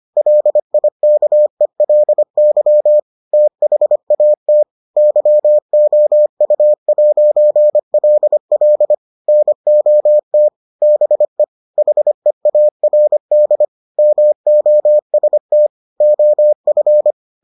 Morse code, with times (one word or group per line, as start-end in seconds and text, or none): 0.2-3.0 LIKELY
3.3-4.6 THAT
5.0-8.9 YOU'LL
9.3-10.5 NOT
10.8-11.4 BE
11.8-13.6 HEARD
14.0-15.7 MOST
16.0-17.1 OF